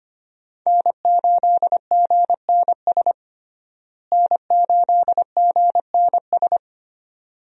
Morse code, heard as a keyed tone, about 25 wpm, 700 Hz. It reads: N8GNS N8GNS
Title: USA Call Signs - 5 Characters - Repeated 2x - 25wpm